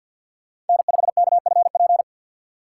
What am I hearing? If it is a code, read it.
Morse code, heard as a keyed tone, about 52 wpm, 700 Hz. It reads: N5KFP
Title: USA Call Signs - 5 Characters - 2x Character Spacing - 50wpm